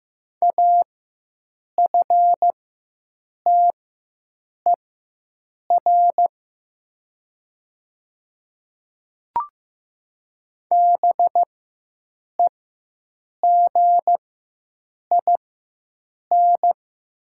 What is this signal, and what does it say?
Morse code, heard as a keyed tone, about 15 wpm, spaced as for 7 wpm, 700 Hz. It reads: AFTER BEGIN
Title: 4x Character Spacing - Top 500 Words - 15wpm